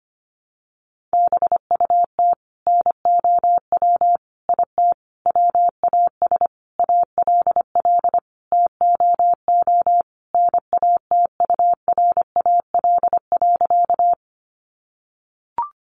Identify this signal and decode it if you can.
Morse code, heard as a keyed tone, about 25 wpm, 700 Hz. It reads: BUT NOW IT WAS ALL TOO NATURAL.
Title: Sentences from Top 700 Words 25wpm